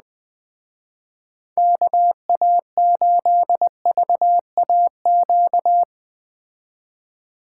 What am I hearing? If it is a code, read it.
Morse code, heard as a keyed tone, about 19 wpm, 700 Hz. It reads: KA8VAQ
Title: Speed Racing - USA 2x3 Call Signs 17wpm